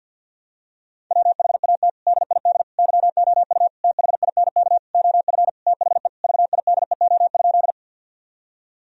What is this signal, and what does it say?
Morse code, heard as a keyed tone, about 50 wpm, 700 Hz. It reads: WHAT DID YOU THINK OF THE VIDEO?
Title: Sentences from Top 1000 Words - Review Part 4 - 50wpm